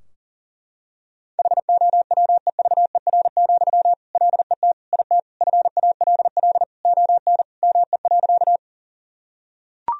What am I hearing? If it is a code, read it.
Morse code, heard as a keyed tone, about 40 wpm, 700 Hz. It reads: HOWEVER, LET IT FALL ON ME.
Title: Sentences from Top 1000 Words - Review Part 2 - 40wpm